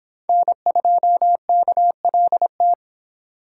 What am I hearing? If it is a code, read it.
Morse code, heard as a keyed tone, about 23 wpm, 700 Hz. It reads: N2KLT